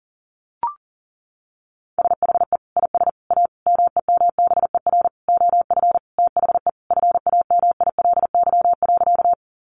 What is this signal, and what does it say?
Morse code, heard as a keyed tone, about 40 wpm, 700 Hz. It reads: SHE IS A MEMBER OF THE FAMILY.
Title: Sentences from Top 1100 Words 40wpm